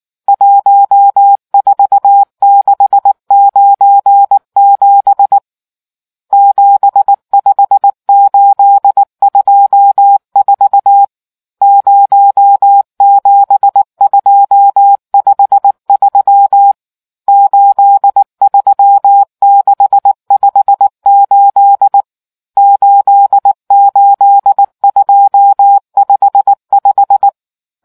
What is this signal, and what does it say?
Morse code, heard as a keyed tone, about 19 wpm, 800 Hz. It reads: 14697 75824 07253 83658 88255